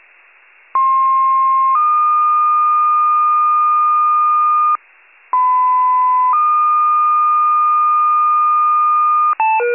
Two-tone page, heard as a one-second tone, then a three-second tone, about 1060 then 1220 Hz, about 1010 then 1220 Hz. It is heard from 0.7 to 9.3 s.
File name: pager / 94 RESCUE OFC / wcalert